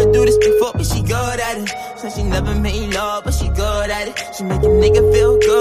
Ringback tone (Europe, 425 Hz), with noise.